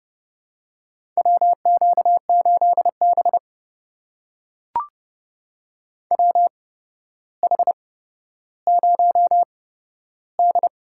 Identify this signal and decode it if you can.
Morse code, heard as a keyed tone, about 29 wpm, spaced as for 13 wpm, 700 Hz. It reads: WQ8B WH0D
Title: USA Call Signs - 4 Characters - 8x Character Spacing - 30wpm